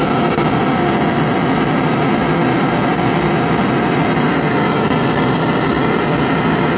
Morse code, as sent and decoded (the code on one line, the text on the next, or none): none